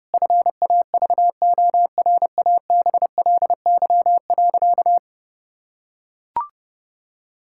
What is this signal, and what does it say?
Morse code, heard as a keyed tone, about 30 wpm, 700 Hz. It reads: FAVORABLY.
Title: Pangram Challenge v2 - 30wpm